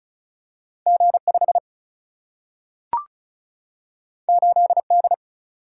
Morse code, heard as a keyed tone, about 36 wpm, 700 Hz. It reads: G 5 8 D
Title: Single-Letter Single-Number 35wpm